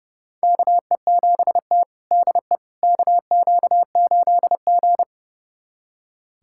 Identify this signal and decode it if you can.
Morse code, heard as a keyed tone, about 31 wpm, 700 Hz. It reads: KE7T DE KQ8G
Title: USA Callsign DE Callsign - 4 Characters - 30wpm